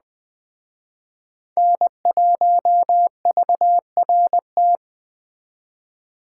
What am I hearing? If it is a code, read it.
Morse code, heard as a keyed tone, about 20 wpm, 700 Hz. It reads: N1VRT